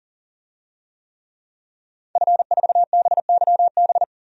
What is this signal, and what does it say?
Morse code, heard as a keyed tone, about 37 wpm, 700 Hz. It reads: F4BYB
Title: DXCC-100 Call Signs - 5 Characters - 40wpm